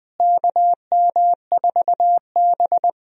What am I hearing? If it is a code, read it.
Morse code, heard as a keyed tone, about 20 wpm, 700 Hz. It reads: KM4B
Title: USA Call Signs - 4 Characters - Repeated 3x - 20wpm